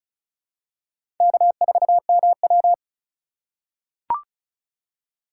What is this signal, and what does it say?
Morse code, heard as a keyed tone, about 34 wpm, 700 Hz. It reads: K4MW
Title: USA Call Signs - 2x Character Spacing - 35wpm